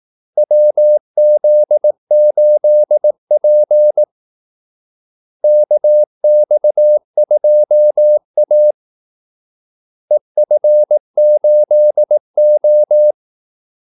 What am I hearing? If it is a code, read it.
Morse code, heard as a keyed tone, about 18 wpm, 600 Hz. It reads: WZ8P KX2A EF8O
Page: Kaldesignaler 18 wpm